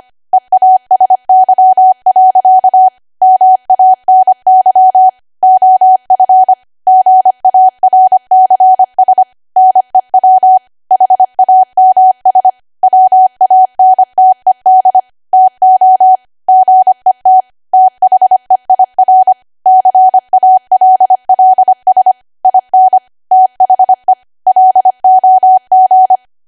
Morse code, heard as a keyed tone, about 25 wpm, 750 Hz. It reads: EASY. MANY OF GARCS NEW HAMS WANTED TO GET THEIR CALLS IN THE LOG
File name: Archive / 25 WPM / 25WPM